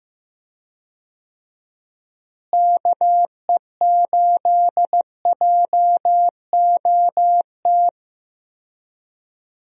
Morse code, heard as a keyed tone, about 16 wpm, 700 Hz. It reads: KE8JOT
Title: USA Call Signs - 6 Characters - Repeated 3x - 15wpm